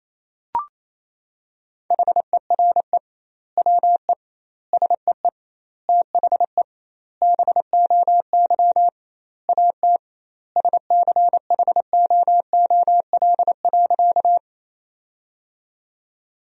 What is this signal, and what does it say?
Morse code, heard as a keyed tone, about 28 wpm, 700 Hz. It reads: HERE WE SEE THE BOY AT SCHOOL.
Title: 2x Word-Spacing - Sentences from Top 300 Words - 28wpm